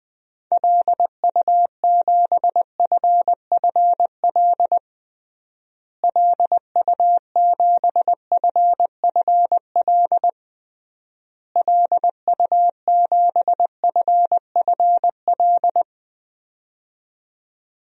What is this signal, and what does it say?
Morse code, heard as a keyed tone, about 21 wpm, 700 Hz. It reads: LU7FFL LU7FFL LU7FFL
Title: DXCC-100 Call Signs - 6 Characters - Repeated 3x - 20wpm